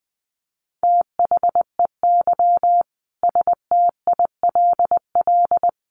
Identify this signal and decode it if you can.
Morse code, heard as a keyed tone, about 20 wpm, 700 Hz. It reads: THEY STILL
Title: Sentences from Top 1200 Words 20wpm